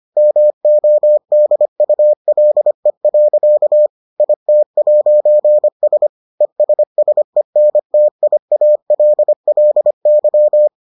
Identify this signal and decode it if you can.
Morse code, heard as a keyed tone, about 25 wpm, 600 Hz. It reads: MODULE. IT'S ESSENTIALLY